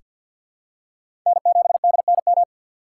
Morse code, heard as a keyed tone, about 47 wpm, 700 Hz. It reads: N7DNM
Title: USA Call Signs - 6x Character Spacing - 50wpm